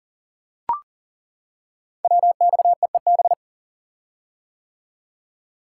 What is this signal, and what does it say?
Morse code, heard as a keyed tone, about 39 wpm, 700 Hz. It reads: WXEEB
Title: Code Groups - 5 Letters - 40wpm